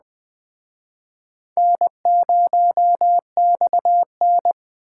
Morse code, heard as a keyed tone, about 24 wpm, 700 Hz. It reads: N0XN
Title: Speed Racing - USA 1x2 Call Signs 20wpm